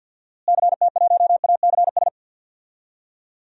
Morse code, heard as a keyed tone, about 53 wpm, 700 Hz. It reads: CT1ACS
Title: DXCC-100 Call Signs - 6 Characters - Repeated 2x - 50wpm